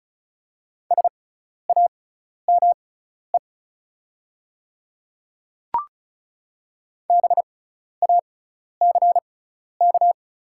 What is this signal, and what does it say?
Morse code, heard as a keyed tone, about 36 wpm, spaced as for 12 wpm, 700 Hz. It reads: SAME BACK
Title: 6x Character Spacing - Top 300 Words - 35wpm